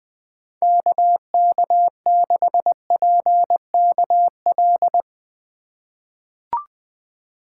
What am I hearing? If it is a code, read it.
Morse code, heard as a keyed tone, about 20 wpm, 700 Hz. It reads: KK6PKL
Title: USA Call Signs - 6 Characters - 2x Character Spacing - 20wpm